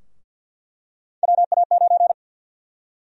Morse code, heard as a keyed tone, about 48 wpm, 700 Hz. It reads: WA9
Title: USA Call Sign Prefixes - 50wpm